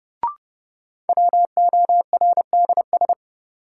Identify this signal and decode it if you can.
Morse code, heard as a keyed tone, about 30 wpm, 700 Hz. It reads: WORDS
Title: Sentences from Top 1000 Words - Review Part 6 - New Word Drill - 30wpm